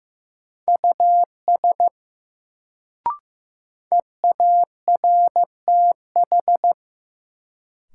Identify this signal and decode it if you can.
Morse code, heard as a keyed tone, about 15 wpm, 700 Hz. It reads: US EARTH
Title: Sentences from Top 200 Words - New Word Drill - 15wpm